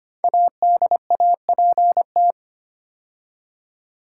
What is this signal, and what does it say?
Morse code, heard as a keyed tone, about 25 wpm, 700 Hz. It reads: ADAPT
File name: 5 Letter Words - First 3 Letters Given - 25wpm